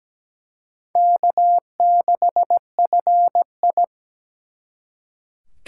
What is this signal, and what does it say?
Morse code, heard as a keyed tone, about 20 wpm, 700 Hz. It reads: K6FI
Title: Speed Racing - USA 1x2 Call Signs 17wpm